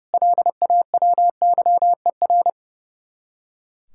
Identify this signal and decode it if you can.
Morse code, heard as a keyed tone, about 38 wpm, 700 Hz. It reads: LAWYER